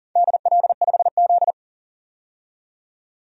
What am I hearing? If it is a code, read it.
Morse code, heard as a keyed tone, about 40 wpm, 700 Hz. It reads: DL5Z
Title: DXCC-100 Call Signs - 4 Characters - 40wpm